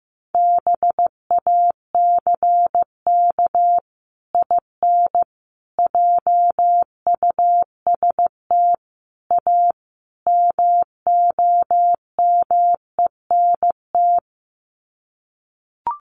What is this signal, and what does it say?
Morse code, heard as a keyed tone, about 14 wpm, 700 Hz. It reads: BACK IN JUST A MOMENT
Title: Sets of 5 Words Encore 15wpm